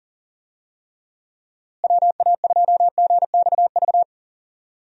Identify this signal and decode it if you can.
Morse code, heard as a keyed tone, about 40 wpm, 700 Hz. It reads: WA2GXV
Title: USA Call Signs - 6 Characters - Repeated 2x - 40wpm